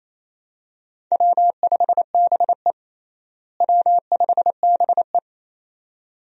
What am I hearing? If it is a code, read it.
Morse code, heard as a keyed tone, about 28 wpm, 700 Hz. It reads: W5BE W5BE
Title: USA Call Signs - 4 Characters - Repeated 2x - 28wpm